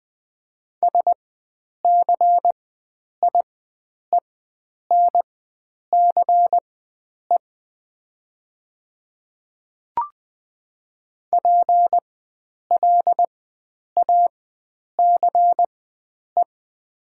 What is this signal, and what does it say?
Morse code, heard as a keyed tone, about 20 wpm, spaced as for 9 wpm, 700 Hz. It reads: SCIENCE PLACE
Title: 4x Character Spacing - Top 500 Words - 20wpm